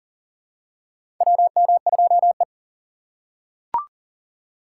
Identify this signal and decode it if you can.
Morse code, heard as a keyed tone, about 37 wpm, 700 Hz. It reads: WM2E
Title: USA Call Signs - 6x Character Spacing - 40wpm